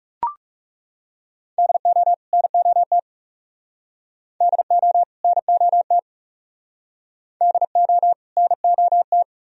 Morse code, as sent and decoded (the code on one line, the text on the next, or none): -.. --- / -. --- - / -.. --- / -. --- - / -.. --- / -. --- -
DO NOT DO NOT DO NOT